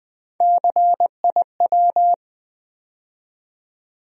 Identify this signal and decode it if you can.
Morse code, heard as a keyed tone, about 20 wpm, 700 Hz. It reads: CIW
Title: USA Call Sign Postfixes - 3 Letters 20wpm